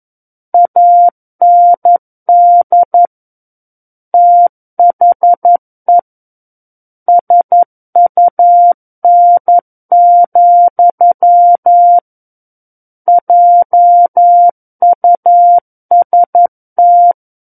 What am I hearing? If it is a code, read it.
Morse code, heard as a keyed tone, about 11 wpm, 700 Hz. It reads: AND THE SUN, JUST
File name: Book One / 02-Chapter 2 - 11 WPM